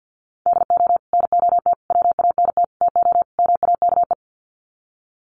Morse code, heard as a keyed tone, about 50 wpm, 700 Hz. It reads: DO NOT WANT TO MAKE
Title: Sets of 5 Words Encore 50wpm